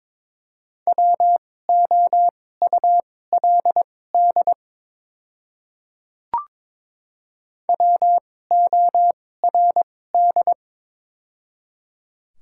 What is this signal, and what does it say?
Morse code, heard as a keyed tone, about 22 wpm, spaced as for 16 wpm, 690 Hz. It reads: WOULD WORD
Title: 2x Character Spacing - Top 100 Words - 22wpm